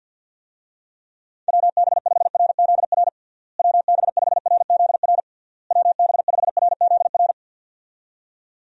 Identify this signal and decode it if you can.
Morse code, heard as a keyed tone, about 51 wpm, 700 Hz. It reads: WB5RZR WB5RZR WB5RZR
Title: USA Call Signs - 6 Characters - Repeated 3x - 50wpm